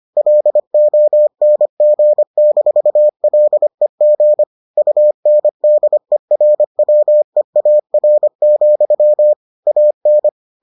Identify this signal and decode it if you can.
Morse code, heard as a keyed tone, about 25 wpm, 600 Hz. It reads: LONG-LEG UNDERWEAR, AN